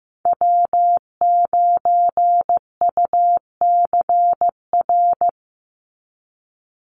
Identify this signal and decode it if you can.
Morse code, heard as a keyed tone, about 15 wpm, 700 Hz. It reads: W9UCR
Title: Call Signs Repeated 3x - 15wpm